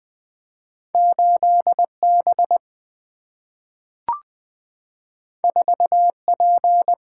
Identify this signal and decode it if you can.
Morse code, heard as a keyed tone, about 20 wpm, 700 Hz. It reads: 8 B 4 P
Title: Two Letter-Number Combinations - 20wpm